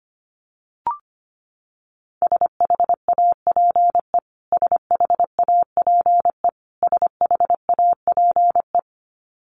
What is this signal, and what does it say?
Morse code, heard as a keyed tone, about 25 wpm, 700 Hz. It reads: SHAPE SHAPE SHAPE
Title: Top 500 Words Repeated 25wpm